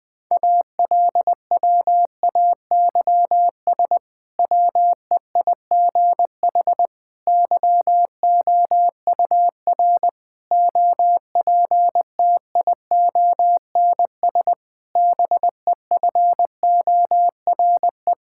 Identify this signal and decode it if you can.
Morse code, heard as a keyed tone, about 21 wpm, 700 Hz. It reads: ALWAYS WEIGH YOUR OPTIONS BEFORE
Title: Difficult Sentences with Hints - 20wpm